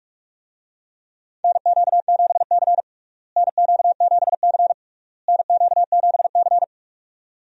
Morse code, heard as a keyed tone, about 44 wpm, 700 Hz. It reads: NQ7C NQ7C NQ7C
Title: USA Call Signs - 4 Characters - Repeated 3x - 45wpm